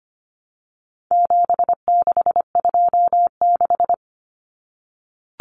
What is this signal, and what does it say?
Morse code, heard as a keyed tone, about 25 wpm, 700 Hz. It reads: 7626
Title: Numbers - 4-Digit 25wpm